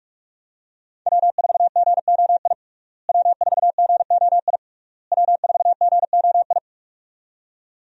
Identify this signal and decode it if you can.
Morse code, heard as a keyed tone, about 44 wpm, 700 Hz. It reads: W4GOI W4GOI W4GOI
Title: USA Call Signs - 5 Characters - Repeated 3x - 45wpm